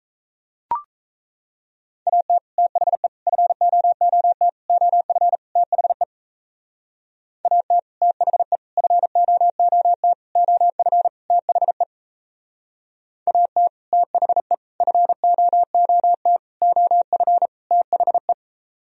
Morse code, heard as a keyed tone, about 36 wpm, 700 Hz. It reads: AT THE FOOT OF THE AT THE FOOT OF THE AT THE FOOT OF THE